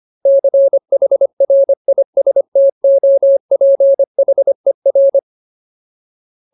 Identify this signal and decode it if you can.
Morse code, heard as a keyed tone, about 25 wpm, 550 Hz. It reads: CHRISTOPHER
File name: Ultimate Word List / 25WPM Ultimate